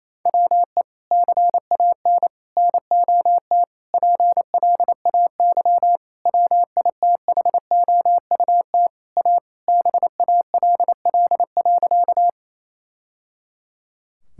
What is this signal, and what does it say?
Morse code, heard as a keyed tone, about 28 wpm, 700 Hz. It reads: WE CAN NOT PLAY WITHOUT A BALL.
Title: Sentences from Top 1000 Words - Review Part 6 - 28wpm